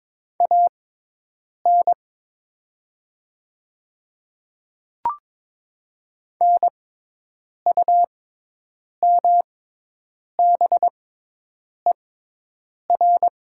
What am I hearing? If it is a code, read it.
Morse code, heard as a keyed tone, about 22 wpm, spaced as for 8 wpm, 700 Hz. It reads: AN NUMBER
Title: 6x Character Spacing - Top 100 Words - 22wpm